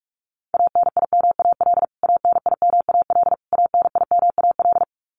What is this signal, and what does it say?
Morse code, heard as a keyed tone, about 44 wpm, 700 Hz. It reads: ANIMAL ANIMAL ANIMAL
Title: Top 200 Words Repeated 45wpm